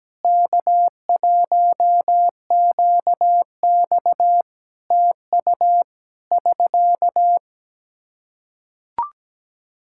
Morse code, heard as a keyed tone, about 17 wpm, 700 Hz. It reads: K1QX TU <SK>
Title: Practice QSO - Exchange 4 - Ending - 17wpm